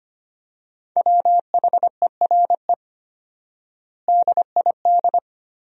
Morse code, heard as a keyed tone, about 25 wpm, 700 Hz. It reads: WHERE DID
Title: 4x Word-Spacing - Sentences from Top 200 Words - 25wpm